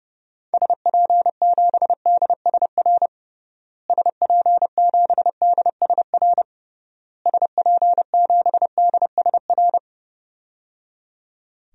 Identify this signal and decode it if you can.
Morse code, heard as a keyed tone, about 30 wpm, 700 Hz. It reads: SP7DSR SP7DSR SP7DSR
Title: DXCC-100 Call Signs - 6 Characters - Repeated 3x - 30wpm